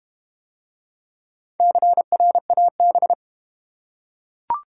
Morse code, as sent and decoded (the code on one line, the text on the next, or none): -.-. .-. .- -...
CRAB